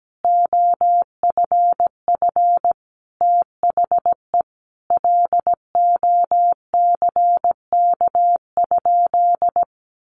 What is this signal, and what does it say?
Morse code, heard as a keyed tone, about 17 wpm, 700 Hz. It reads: OFF THE LOCK?
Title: Sentences from Top 1300 Words 17wpm